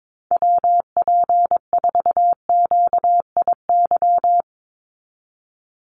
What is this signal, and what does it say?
Morse code, heard as a keyed tone, about 22 wpm, 700 Hz. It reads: WP4QIY
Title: USA Call Signs 22wpm